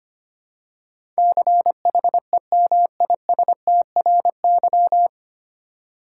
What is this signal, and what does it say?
Morse code, heard as a keyed tone, about 25 wpm, 700 Hz. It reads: CHEMISTRY
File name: Nine Letter Words - 25wpm